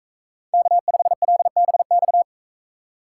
Morse code, heard as a keyed tone, about 39 wpm, 700 Hz. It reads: K5LBX